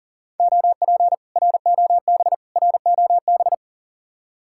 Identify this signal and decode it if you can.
Morse code, heard as a keyed tone, about 52 wpm, 700 Hz. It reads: OP ROB ROB